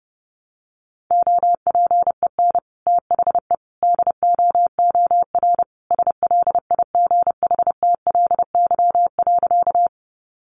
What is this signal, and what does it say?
Morse code, heard as a keyed tone, about 30 wpm, 700 Hz. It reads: OPEN THE DOOR SLIGHTLY.
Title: Sentences from Top 1300 Words 30wpm